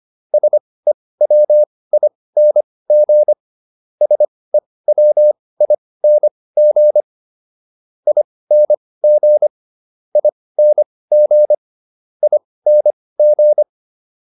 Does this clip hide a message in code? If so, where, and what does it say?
Yes: Morse code from 0.3 to 13.6 s, heard as a keyed tone, about 25 wpm, spaced as for 18 wpm, 600 Hz. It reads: SEWING SEWING ING ING ING